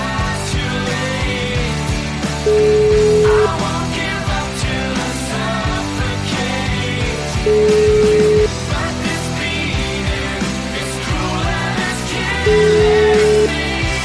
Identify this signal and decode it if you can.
Ringback tone (Europe, 425 Hz), with noise.